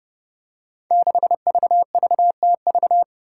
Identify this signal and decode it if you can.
Morse code, heard as a keyed tone, about 32 wpm, 700 Hz. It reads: 6VVTV
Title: Five Challenging Letter-Number Combinations 30wpm